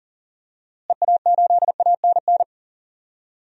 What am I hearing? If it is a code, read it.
Morse code, heard as a keyed tone, about 37 wpm, 700 Hz. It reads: EA8ANN
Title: DXCC-100 Call Signs - 6 Characters - Repeated 2x - 40wpm